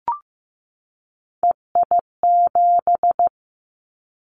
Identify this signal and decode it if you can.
Morse code, heard as a keyed tone, about 15 wpm, 700 Hz. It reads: EI7
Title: Callsign Prefixes for Top 50 DXCC - 15wpm